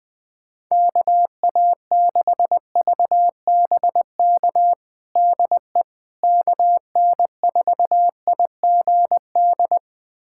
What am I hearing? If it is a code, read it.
Morse code, heard as a keyed tone, about 20 wpm, 700 Hz. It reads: KA6VBK DE KN4IGD
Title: USA Callsign DE Callsign 20wpm